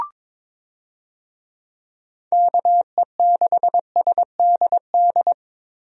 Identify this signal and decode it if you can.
Morse code, heard as a keyed tone, about 22 wpm, 700 Hz. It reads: KE6SDD
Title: USA Call Signs - 6 Characters - 22wpm